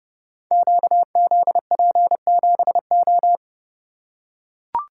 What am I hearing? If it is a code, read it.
Morse code, heard as a keyed tone, about 29 wpm, 700 Hz. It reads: QZP7O
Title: Code Groups - 5 Letter:Numbers - 30wpm